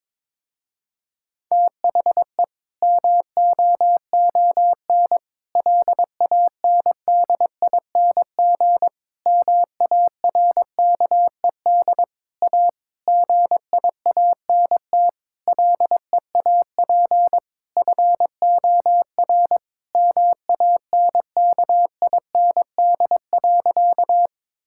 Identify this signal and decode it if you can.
Morse code, heard as a keyed tone, about 22 wpm, 700 Hz. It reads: THE MOON LANDING MARKED A GIANT LEAP FOR MANKIND.
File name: Difficult Sentences with Hints - 22wpm